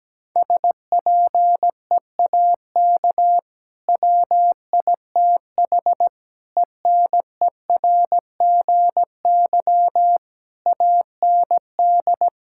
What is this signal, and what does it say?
Morse code, heard as a keyed tone, about 17 wpm, 700 Hz. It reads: SPEAK WITH ENERGY AND
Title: Sentences from Top 1000 Words - Review Part 6 - 17wpm